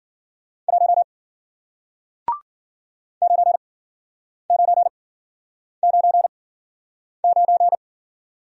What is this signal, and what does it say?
Morse code, heard as a keyed tone, about 52 wpm, spaced as for 9 wpm, 700 Hz. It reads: 1 9999